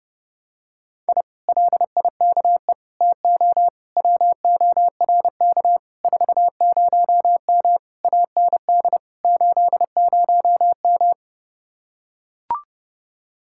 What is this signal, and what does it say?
Morse code, heard as a keyed tone, about 29 wpm, 700 Hz. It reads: I LIKE TO WORK 40M AND 80M